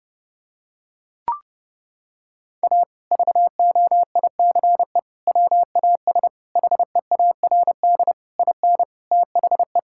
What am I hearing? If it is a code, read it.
Morse code, heard as a keyed tone, about 30 wpm, 700 Hz. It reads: A VOICE WAS HEARD IN THE
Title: Sentences from Top 1000 Words - Review Part 5 - 30wpm